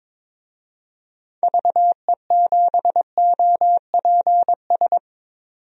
Morse code, heard as a keyed tone, about 22 wpm, 700 Hz. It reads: VE7OPS
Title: DXCC-100 Call Signs - 6 Characters - 22wpm